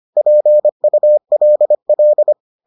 Morse code, heard as a keyed tone, about 25 wpm, 600 Hz. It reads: PULL